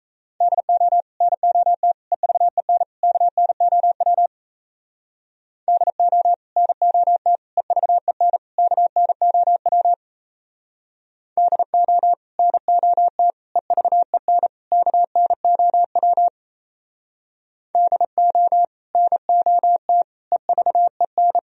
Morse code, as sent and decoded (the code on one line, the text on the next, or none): -.. --- / -. --- - / . ...- . -. / -.- -. --- .-- / -.. --- / -. --- - / . ...- . -. / -.- -. --- .-- / -.. --- / -. --- - / . ...- . -. / -.- -. --- .-- / -.. --- / -. --- - / . ...- . -.
DO NOT EVEN KNOW DO NOT EVEN KNOW DO NOT EVEN KNOW DO NOT EVEN